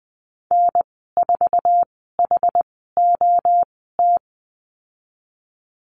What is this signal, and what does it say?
Morse code, heard as a keyed tone, about 20 wpm, spaced as for 14 wpm, 700 Hz. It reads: N4HOT
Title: 076 Lesson 13g - Letter L - Call Signs 20wpm